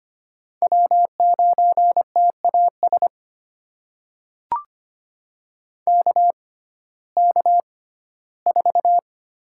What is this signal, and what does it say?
Morse code, heard as a keyed tone, about 25 wpm, spaced as for 15 wpm, 700 Hz. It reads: W9TAS KK4
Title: USA Call Signs - 5 Characters - 6x Character Spacing - 25wpm